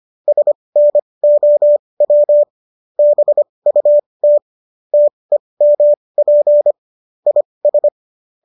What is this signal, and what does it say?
Morse code, heard as a keyed tone, about 25 wpm, spaced as for 20 wpm, 590 Hz. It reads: SNOW BUT TEMP IS
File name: qso221 20